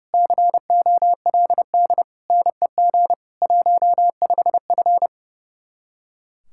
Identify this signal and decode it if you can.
Morse code, heard as a keyed tone, about 36 wpm, 700 Hz. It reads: COLD NEG 15F
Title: Speed-Racing Common QSO Phrases - 30wpm